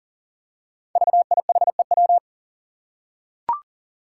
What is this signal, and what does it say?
Morse code, heard as a keyed tone, about 42 wpm, 700 Hz. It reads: VIHEW
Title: Code Groups - 5 Letter:Numbers - 40wpm